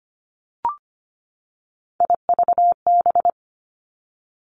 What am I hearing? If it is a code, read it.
Morse code, heard as a keyed tone, about 25 wpm, 700 Hz. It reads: IVB
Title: Three Challenging Letter-Number Combinations 25wpm